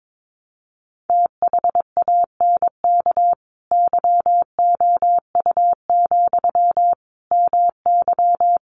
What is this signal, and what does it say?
Morse code, heard as a keyed tone, about 22 wpm, 700 Hz. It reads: THANK YOU, MY
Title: Sentences from Top 600 Words 22wpm